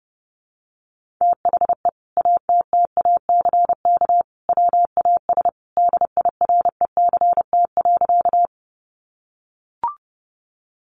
Morse code, heard as a keyed tone, about 30 wpm, 700 Hz. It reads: THE ATTACK WAS DIRECT.
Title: Sentences from Top 1500 Words 30wpm